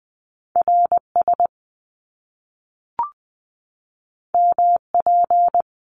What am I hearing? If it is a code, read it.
Morse code, heard as a keyed tone, about 20 wpm, 700 Hz. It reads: RS MP
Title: Top Two Letter Combinations in Words 20wpm